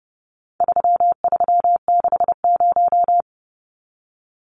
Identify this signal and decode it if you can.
Morse code, heard as a keyed tone, about 30 wpm, 700 Hz. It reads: 3360
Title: Numbers - 4-Digit 30wpm